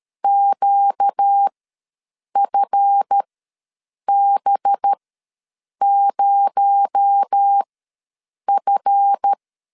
Morse code, heard as a keyed tone, about 13 wpm, spaced as for 7 wpm, 800 Hz. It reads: QFB0F
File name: LEZIONI / 7 GF9Z / 7 GF ws 12wpm 3140